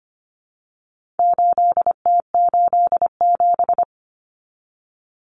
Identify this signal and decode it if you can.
Morse code, heard as a keyed tone, about 25 wpm, 700 Hz. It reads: 8T87
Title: Common Cut Numbers - 4-Digit 25wpm